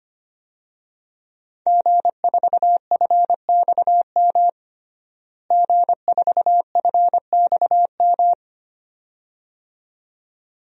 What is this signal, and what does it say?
Morse code, heard as a keyed tone, about 25 wpm, 700 Hz. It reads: G4FXM G4FXM
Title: DXCC-100 Call Signs - 5 Characters - Repeated 2x - 25wpm